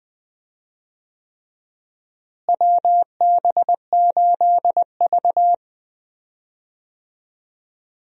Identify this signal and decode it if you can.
Morse code, heard as a keyed tone, about 20 wpm, 700 Hz. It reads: WB8V
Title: USA Call Signs - 4 Characters - Repeated 3x - 20wpm